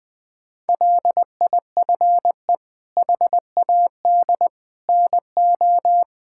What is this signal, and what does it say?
Morse code, heard as a keyed tone, about 20 wpm, 700 Hz. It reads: LIFE HAD NO
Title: Sentences from Top 1000 Words - Review Part 6 - 20wpm